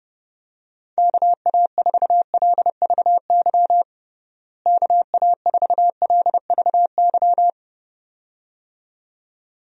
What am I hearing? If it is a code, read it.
Morse code, heard as a keyed tone, about 30 wpm, 700 Hz. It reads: KA4LVY KA4LVY
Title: USA Call Signs - 6 Characters - Repeated 2x - 30wpm